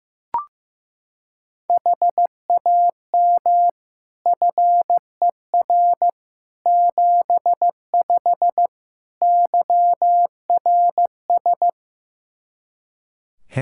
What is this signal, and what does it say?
Morse code, heard as a keyed tone, about 15 wpm, 700 Hz. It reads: HAM FER 75 YRS
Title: K1USN SST QSO Practice 15wpm